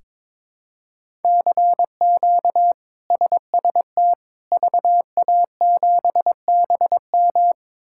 Morse code, heard as a keyed tone, about 22 wpm, 700 Hz. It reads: CQ SST VA7BM
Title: K1USN SST QSO Practice - 22wpm